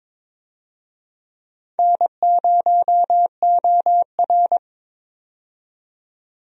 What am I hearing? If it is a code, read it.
Morse code, heard as a keyed tone, about 21 wpm, 700 Hz. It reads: N0OR
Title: USA Call Signs - 4 Characters - 22wpm